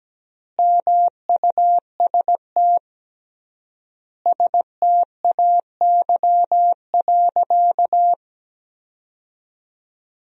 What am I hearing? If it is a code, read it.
Morse code, heard as a keyed tone, about 17 wpm, 700 Hz. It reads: MUST STAY.
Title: 3x Word-Spacing - Sentences from Top 500 Words - 17wpm